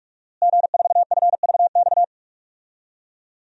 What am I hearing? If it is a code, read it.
Morse code, heard as a keyed tone, about 43 wpm, 700 Hz. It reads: G4FVX
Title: DXCC-100 Call Signs - 5 Characters - Repeated 2x - 45wpm